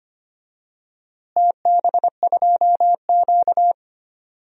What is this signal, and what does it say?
Morse code, heard as a keyed tone, about 25 wpm, 700 Hz. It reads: TB2Q
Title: DXCC-100 Call Signs - 4 Characters - Repeated 3x - 25wpm